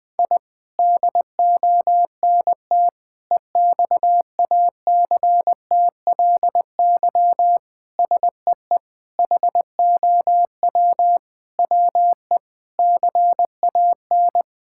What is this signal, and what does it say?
Morse code, heard as a keyed tone, about 20 wpm, 700 Hz. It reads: I DONT EXACTLY SEE HOW WE CAN
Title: Sentences from Top 1000 Words - Review Part 6 - 20wpm